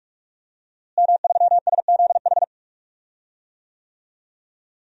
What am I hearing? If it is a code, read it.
Morse code, heard as a keyed tone, about 41 wpm, 700 Hz. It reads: M3SZH
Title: DXCC-100 Call Signs - 5 Characters - Repeated 3x - 45wpm